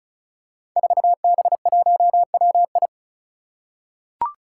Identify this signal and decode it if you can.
Morse code, heard as a keyed tone, about 36 wpm, 700 Hz. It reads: 4B1WI
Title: Five Challenging Letter-Number Combinations 35wpm